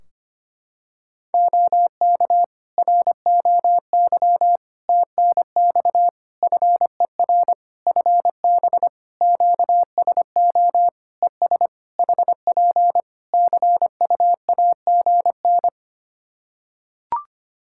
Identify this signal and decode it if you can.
Morse code, heard as a keyed tone, about 25 wpm, 700 Hz. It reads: OK ROY TNX FER FB QSO ES HP CUAGN
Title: Practice QSO - Exchange 4 - Ending - 25wpm